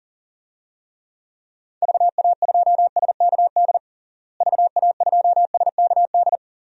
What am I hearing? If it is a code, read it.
Morse code, heard as a keyed tone, about 37 wpm, 700 Hz. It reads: VA2SKD VA2SKD
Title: DXCC-100 Call Signs - 6 Characters - Repeated 2x - 40wpm